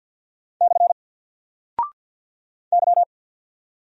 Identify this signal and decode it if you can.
Morse code, heard as a keyed tone, about 48 wpm, 700 Hz. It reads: / Y
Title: Single Character (Letter:Number:Punctuation:Prosign) Alt 50wpm